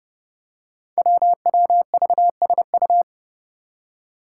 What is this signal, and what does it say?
Morse code, heard as a keyed tone, about 30 wpm, 700 Hz. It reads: WWVSU
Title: Five Challenging Letter-Number Combinations 30wpm